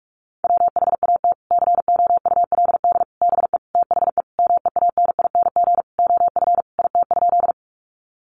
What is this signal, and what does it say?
Morse code, heard as a keyed tone, about 44 wpm, 700 Hz. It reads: WHAT COULD BE THE MEANING OF IT?
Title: Sentences from Top 900 Words 45wpm